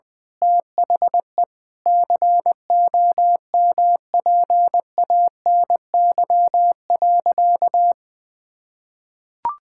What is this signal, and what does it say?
Morse code, heard as a keyed tone, about 20 wpm, 700 Hz. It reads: THE COMPANY.
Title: Sentences with 8-Word Phrases - 20wpm